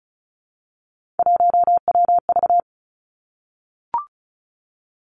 Morse code, heard as a keyed tone, about 34 wpm, 700 Hz. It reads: 1WV
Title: Three Challenging Letter-Number Combinations 35wpm